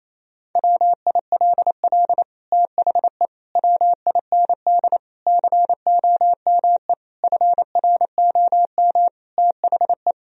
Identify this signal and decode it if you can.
Morse code, heard as a keyed tone, about 27 wpm, 700 Hz. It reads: WILL THE WIND COME FROM THE
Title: Sentences from Top 1000 Words - Review Part 2 - 28wpm